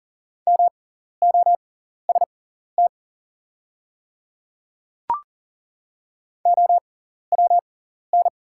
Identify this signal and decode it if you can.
Morse code, heard as a keyed tone, about 39 wpm, spaced as for 14 wpm, 700 Hz. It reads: MOST OWN
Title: 6x Character Spacing - Top 300 Words - 40wpm